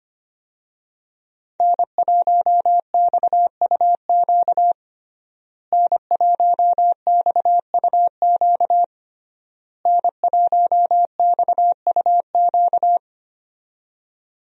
Morse code, heard as a keyed tone, about 25 wpm, 700 Hz. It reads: N1XUQ N1XUQ N1XUQ
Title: USA Call Signs - 5 Characters - Repeated 3x - 25wpm